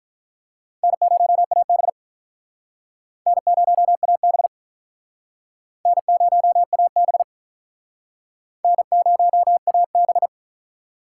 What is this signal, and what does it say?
Morse code, heard as a keyed tone, about 43 wpm, 700 Hz. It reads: N0AB N0AB N0AB N0AB
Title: Speed Racing - USA 1x2 Call Signs 35wpm